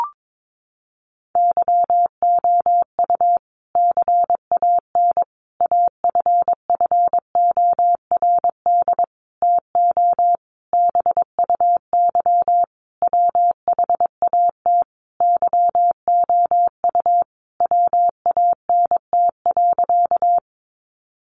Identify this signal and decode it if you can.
Morse code, heard as a keyed tone, about 22 wpm, 700 Hz. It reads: YOU CAN AFFORD TO BUY WHAT YOU WANT.
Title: Sentences from Top 1100 Words 22wpm